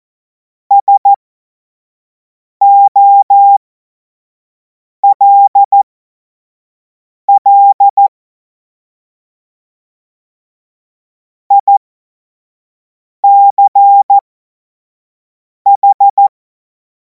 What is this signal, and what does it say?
Morse code, heard as a keyed tone, about 14 wpm, spaced as for 5 wpm, 800 Hz. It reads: SOLL ICH
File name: Morsekurs / Witze / Witz 10. Februar 2026 25 F70